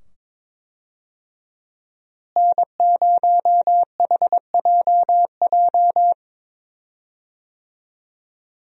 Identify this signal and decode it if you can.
Morse code, heard as a keyed tone, about 22 wpm, 700 Hz. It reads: N0HJJ
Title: USA Call Signs - 5 Characters - Repeated 2x - 22wpm